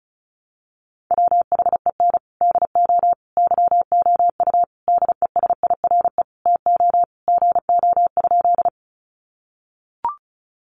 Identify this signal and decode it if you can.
Morse code, heard as a keyed tone, about 34 wpm, 700 Hz. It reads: WHEN DO YOU DESIRE TO GO?
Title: Sentences from Top 1100 Words 35wpm